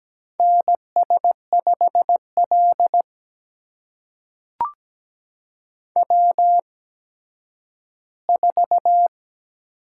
Morse code, heard as a keyed tone, about 17 wpm, spaced as for 10 wpm, 700 Hz. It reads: NS5L W4
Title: USA Call Signs - 8x Character Spacing - 17wpm